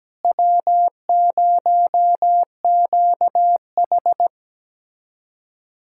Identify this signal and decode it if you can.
Morse code, heard as a keyed tone, about 17 wpm, 700 Hz. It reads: W0QH
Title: USA Callsign DE Callsign - 4 Characters - 17wpm